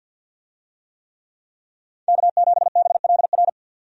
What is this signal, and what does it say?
Morse code, heard as a keyed tone, about 51 wpm, 700 Hz. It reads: K7BLR
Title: USA Call Signs - 5 Characters - 50wpm